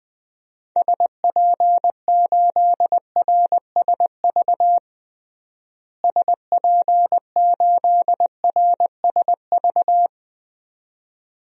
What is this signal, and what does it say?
Morse code, heard as a keyed tone, about 20 wpm, 700 Hz. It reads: SP8RSV SP8RSV
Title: DXCC-100 Call Signs - 6 Characters - Repeated 2x - 20wpm